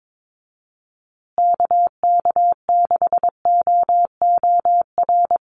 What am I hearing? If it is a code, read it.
Morse code, heard as a keyed tone, about 22 wpm, 700 Hz. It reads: KK6OOR
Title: USA Call Signs Repeated 3x 22wpm